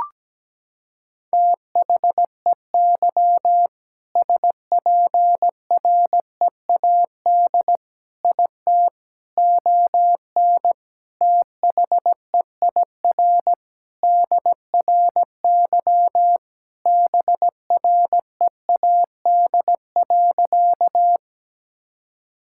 Morse code, heard as a keyed tone, about 17 wpm, 700 Hz. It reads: THEY SPREAD IT ON THEIR DRY BREAD.
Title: Sentences from Top 1600 Words 17wpm